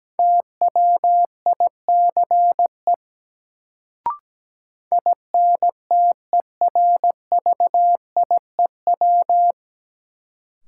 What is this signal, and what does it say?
Morse code, heard as a keyed tone, about 17 wpm, 700 Hz. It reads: TWICE INTERVIEW
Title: Sentences from Top 1600 Words - New Word Drill - 17wpm